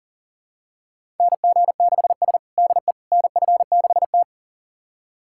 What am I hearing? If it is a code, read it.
Morse code, heard as a keyed tone, about 40 wpm, 700 Hz. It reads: NG6S DE NF6T
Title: USA Callsign DE Callsign - 4 Characters - 40wpm